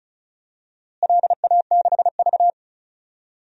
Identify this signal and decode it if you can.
Morse code, heard as a keyed tone, about 36 wpm, 700 Hz. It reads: LA6V
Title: DXCC-100 Call Signs - 4 Characters - Repeated 2x - 35wpm